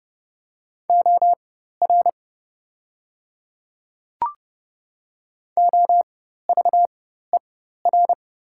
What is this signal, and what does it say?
Morse code, heard as a keyed tone, about 30 wpm, spaced as for 14 wpm, 700 Hz. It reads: OR OVER
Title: 4x Character Spacing - Top 200 Words - 30wpm